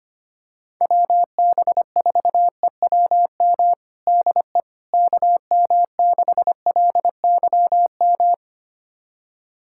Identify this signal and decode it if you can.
Morse code, heard as a keyed tone, about 25 wpm, 700 Hz. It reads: WB4EWM DE KM6LYM
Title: USA Callsign DE Callsign 25wpm